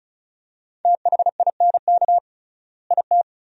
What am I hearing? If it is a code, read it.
Morse code, heard as a keyed tone, about 34 wpm, 700 Hz. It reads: THINK IT
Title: 3x Word-Spacing - Sentences from Top 200 Words - 35wpm